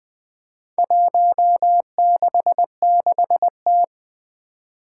Morse code, heard as a keyed tone, about 20 wpm, 700 Hz. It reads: 166T
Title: Four Challenging Letter-Number Combinations 20wpm